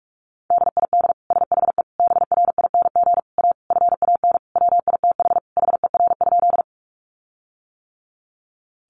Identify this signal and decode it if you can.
Morse code, heard as a keyed tone, about 44 wpm, 700 Hz. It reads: DID SHE BRING A FAN WITH HER?
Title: Sentences from Top 1300 Words 45wpm